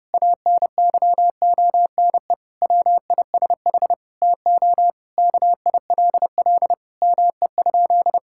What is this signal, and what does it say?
Morse code, heard as a keyed tone, about 30 wpm, 700 Hz. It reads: ANYONE WISH TO KILL ME?
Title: Sentences from Top 1600 Words 30wpm